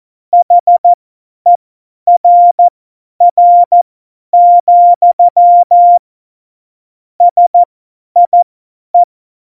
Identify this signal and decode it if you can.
Morse code, heard as a keyed tone, about 14 wpm, spaced as for 10 wpm, 700 Hz. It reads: HERR, SIE
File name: Morsekurs / Witze / Witz 02. Juli 2019, 50 F70